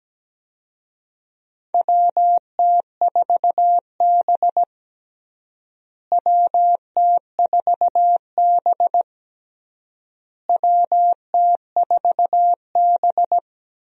Morse code, heard as a keyed tone, about 17 wpm, 700 Hz. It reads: WT4B WT4B WT4B
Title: USA Call Signs - 4 Characters - Repeated 3x - 17wpm